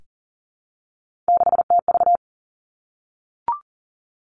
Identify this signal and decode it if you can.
Morse code, heard as a keyed tone, about 40 wpm, 700 Hz. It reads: 6TV
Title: Three Challenging Letter-Number Combinations 40wpm